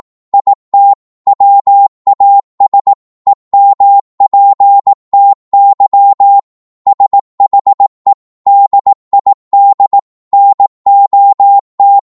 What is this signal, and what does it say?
Morse code, heard as a keyed tone, about 18 wpm, 800 Hz. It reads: IT WAS EMPTY SHE DID NOT